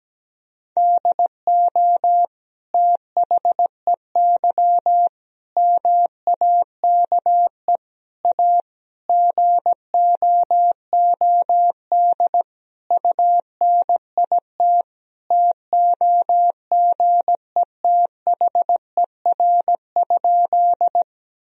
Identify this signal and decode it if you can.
Morse code, heard as a keyed tone, about 17 wpm, 700 Hz. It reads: DO THEY MAKE A GOOD UNIT TOGETHER?
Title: Sentences from Top 1000 Words - Review Part 2 - 17wpm